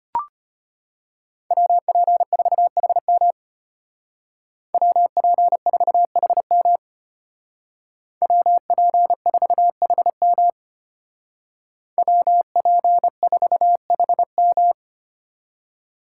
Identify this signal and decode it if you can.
Morse code, heard as a keyed tone, about 33 wpm, 700 Hz. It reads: WP4HM WP4HM WP4HM WP4HM